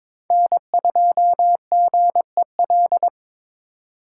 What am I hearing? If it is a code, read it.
Morse code, heard as a keyed tone, about 26 wpm, 700 Hz. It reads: N2GEL
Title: Speed Racing - USA 1x3 Call Signs 22wpm